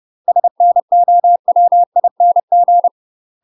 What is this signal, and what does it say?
Morse code, heard as a keyed tone, about 30 wpm, 700 Hz. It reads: SNOWING
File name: UK MASTER QSOs TRAINING / 30WPM / UK MASTER QSOs 0044